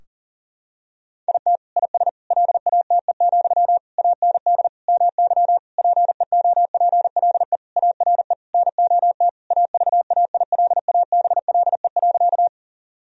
Morse code, heard as a keyed tone, about 37 wpm, 700 Hz. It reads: IT IS LATE, AND MY PEOPLE ARE NOT AVAILABLE.
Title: Sentences from Top 1000 Words - Review Part 3 - 40wpm